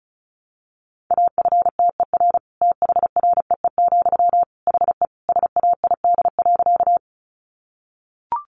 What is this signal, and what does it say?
Morse code, heard as a keyed tone, about 34 wpm, 700 Hz. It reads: AFTER THREE, HE SAID.
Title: Sentences from Top 200 Words 35wpm